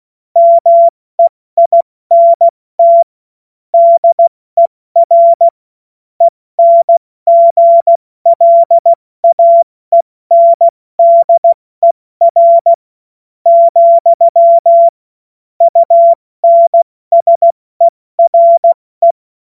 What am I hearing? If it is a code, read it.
Morse code, heard as a keyed tone, about 16 wpm, spaced as for 14 wpm, 680 Hz. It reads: MEINT DER ENGLAENDER, UNSERE
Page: cw-witze